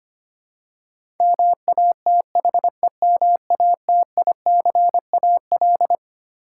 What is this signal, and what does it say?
Morse code, heard as a keyed tone, about 25 wpm, 700 Hz. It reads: MATHEMATICAL
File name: practice / 25wpm / Twelve Letter Words - 25wpm